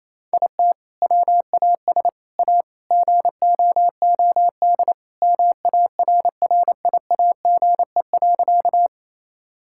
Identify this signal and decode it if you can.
Morse code, heard as a keyed tone, about 28 wpm, 700 Hz. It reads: IT WAS A GOOD MARRIAGE.
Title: Sentences from Top 1000 Words - Review Part 5 - 28wpm